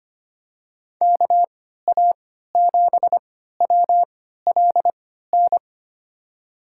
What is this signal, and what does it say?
Morse code, heard as a keyed tone, about 25 wpm, spaced as for 14 wpm, 700 Hz. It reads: KA7WLN
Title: USA Call Signs - 3x Character Spacing - 25wpm